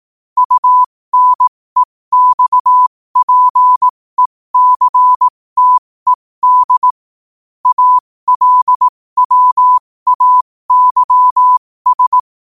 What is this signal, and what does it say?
Morse code, heard as a keyed tone, about 18 wpm, spaced as for 15 wpm, 1000 Hz. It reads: UNEXPECTED ALWAYS
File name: qotdfiles / 15wpm